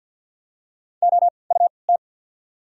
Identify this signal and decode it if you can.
Morse code, heard as a keyed tone, about 53 wpm, 690 Hz. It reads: O U T
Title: 3x Character Spacing - Top 200 Words - 50wpm